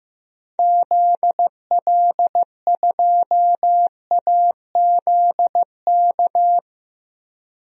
Morse code, heard as a keyed tone, about 16 wpm, 700 Hz. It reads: ZL2AZK
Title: DXCC-100 Call Signs - 6 Characters - 15wpm